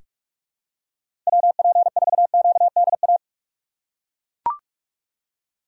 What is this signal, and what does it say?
Morse code, heard as a keyed tone, about 42 wpm, 700 Hz. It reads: WP4QDA
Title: USA Call Signs - 3x Character Spacing - 45wpm